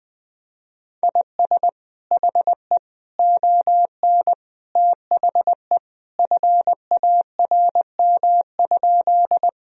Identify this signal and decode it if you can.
Morse code, heard as a keyed tone, about 20 wpm, 700 Hz. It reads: IS HE ON THE FARM?
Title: Sentences from Top 1000 Words - Review Part 2 - 20wpm